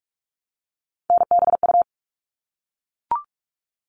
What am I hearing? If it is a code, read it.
Morse code, heard as a keyed tone, about 44 wpm, 700 Hz. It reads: NBU
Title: All Cut Numbers - 3-Digit 45wpm